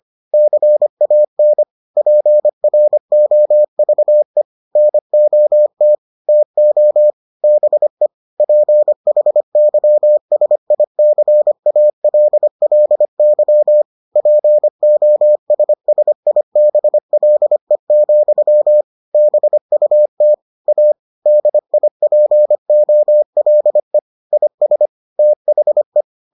Morse code, heard as a keyed tone, about 25 wpm, 600 Hz. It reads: CAN PROVE NOT TO BE PHYSICALLY POSSIBLE, BUT A DIPOLE IS THE